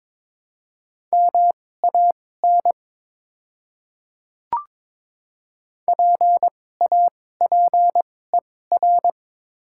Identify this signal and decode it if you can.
Morse code, heard as a keyed tone, about 22 wpm, spaced as for 16 wpm, 700 Hz. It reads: MAN PAPER
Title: 2x Character Spacing - Top 300 Words - 22wpm